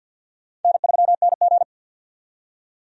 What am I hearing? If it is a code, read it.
Morse code, heard as a keyed tone, about 52 wpm, 700 Hz. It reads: N3NG
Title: USA Call Signs - 4 Characters - 50wpm